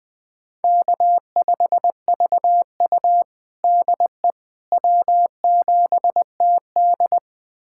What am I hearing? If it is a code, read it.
Morse code, heard as a keyed tone, about 21 wpm, 700 Hz. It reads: K5VU DE W7TD
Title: USA Callsign DE Callsign - 4 Characters - 20wpm